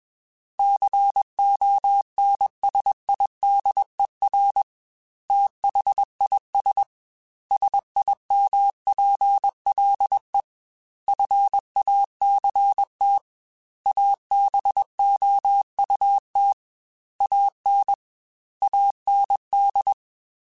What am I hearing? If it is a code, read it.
Morse code, heard as a keyed tone, about 21 wpm, 800 Hz. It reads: CONSIDER THIS SIMPLE FACT ABOUT AN AND
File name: Technical Subjects / Technical Subjects-mp3s / AND Gate @20